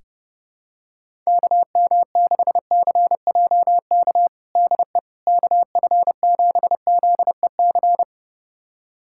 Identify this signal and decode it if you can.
Morse code, heard as a keyed tone, about 30 wpm, 700 Hz. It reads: KM6CJK DE KF7ZEC
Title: USA Callsign DE Callsign - 6 Characters - 30wpm